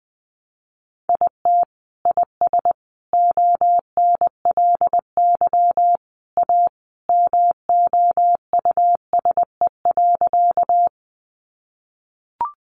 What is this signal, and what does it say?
Morse code, heard as a keyed tone, about 20 wpm, 700 Hz. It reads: IT IS ONLY A MOUSE.
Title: Sentences from Top 700 Words 20wpm